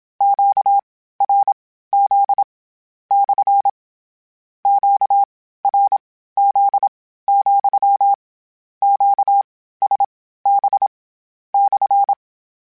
Morse code, heard as a keyed tone, about 27 wpm, spaced as for 15 wpm, 800 Hz. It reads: QRZ / QRZ, QSB /
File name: Q Calls / Q Calls-mp3s / Q Calls SPSL2 @25